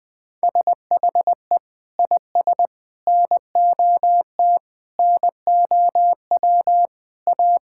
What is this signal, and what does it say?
Morse code, heard as a keyed tone, about 20 wpm, 700 Hz. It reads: SHE IS NOT NOW A
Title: Sentences from Top 1600 Words 20wpm